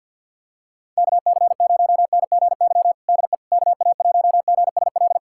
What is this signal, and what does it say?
Morse code, heard as a keyed tone, about 52 wpm, 700 Hz. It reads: KC0NGY DE KA1GSL
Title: USA Callsign DE Callsign - 6 Characters - 50wpm